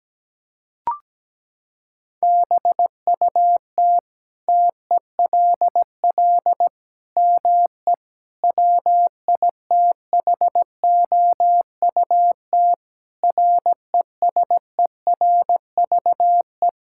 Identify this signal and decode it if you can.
Morse code, heard as a keyed tone, about 17 wpm, 700 Hz. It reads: BUT TELL ME WITHOUT RESERVE
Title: Sentences from Top 1000 Words - Review Part 6 - 17wpm